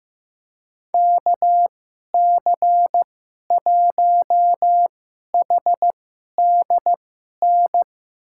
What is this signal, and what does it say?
Morse code, heard as a keyed tone, about 15 wpm, spaced as for 11 wpm, 700 Hz. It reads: KC1HDN
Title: USA Call Signs - 6 Characters - 2x Character Spacing - 15wpm